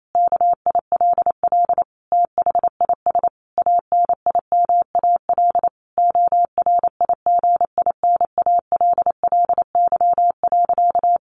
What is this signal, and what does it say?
Morse code, heard as a keyed tone, about 28 wpm, 700 Hz. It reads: KILL THIS ANIMAL ORIGINALLY.
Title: A Sound Of Thunder 28wpm